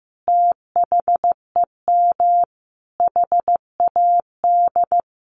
Morse code, heard as a keyed tone, about 15 wpm, 700 Hz. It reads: THEM HAD
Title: Sentences from Top 1100 Words 15wpm